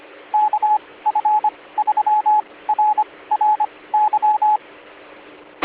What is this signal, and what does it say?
Morse code, heard as a keyed tone, about 25 wpm, spaced as for 18 wpm, 850 Hz. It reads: KF3RRY